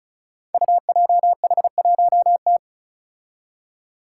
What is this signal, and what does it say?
Morse code, heard as a keyed tone, about 36 wpm, 700 Hz. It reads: UJH1T
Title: Five Challenging Letter-Number Combinations 35wpm